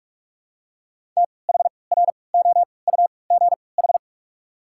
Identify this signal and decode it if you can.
Morse code, heard as a keyed tone, about 46 wpm, spaced as for 25 wpm, 700 Hz. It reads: THROUGH
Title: 3x Character Spacing - Top 300 Words - 45wpm